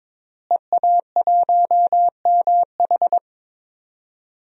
Morse code, heard as a keyed tone, about 22 wpm, 700 Hz. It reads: EA1MH
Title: DXCC-100 Call Signs - 5 Characters - Repeated 3x - 22wpm